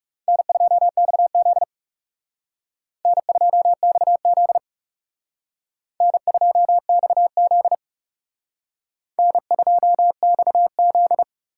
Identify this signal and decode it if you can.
Morse code, heard as a keyed tone, about 37 wpm, 700 Hz. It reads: N2XZ N2XZ N2XZ N2XZ
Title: Speed Racing - USA 1x2 Call Signs 30wpm